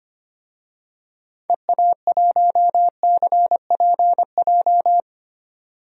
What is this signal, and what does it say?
Morse code, heard as a keyed tone, about 25 wpm, 700 Hz. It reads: EA1CPJ
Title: DXCC-100 Call Signs - 6 Characters - 25wpm